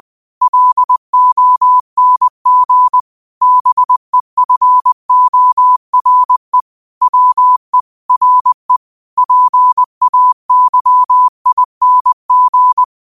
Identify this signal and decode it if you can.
Morse code, heard as a keyed tone, about 20 wpm, 1000 Hz. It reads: LONG BEFORE WE RE PAYING
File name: qotdfiles / 20wpm